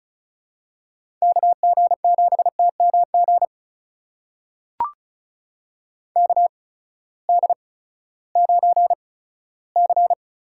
Morse code, heard as a keyed tone, about 36 wpm, spaced as for 18 wpm, 700 Hz. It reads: KG7TMG KD9C
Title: USA Call Signs - 8x Character Spacing - 35wpm